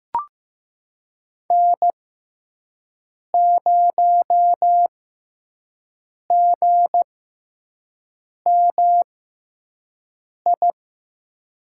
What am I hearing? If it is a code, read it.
Morse code, heard as a keyed tone, about 16 wpm, spaced as for 5 wpm, 700 Hz. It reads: N0GMI
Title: USA Call Signs - 6x Character Spacing - 15wpm